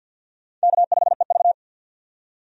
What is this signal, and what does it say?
Morse code, heard as a keyed tone, about 50 wpm, 700 Hz. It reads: K5EV
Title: USA Call Signs - 4 Characters - 50wpm